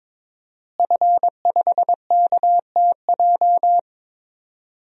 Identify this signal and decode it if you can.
Morse code, heard as a keyed tone, about 22 wpm, 700 Hz. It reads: F5KTJ
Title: DXCC-100 Call Signs - 5 Characters - Repeated 3x - 22wpm